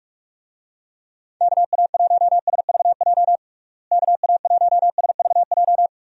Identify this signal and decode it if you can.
Morse code, heard as a keyed tone, about 46 wpm, 700 Hz. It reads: KA1SVJ KA1SVJ
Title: USA Call Signs - 6 Characters - Repeated 2x - 45wpm